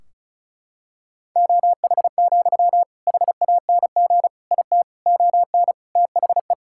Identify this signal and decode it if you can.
Morse code, heard as a keyed tone, about 34 wpm, 700 Hz. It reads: OH, HANG IT ON THE
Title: Sentences from Top 1600 Words 35wpm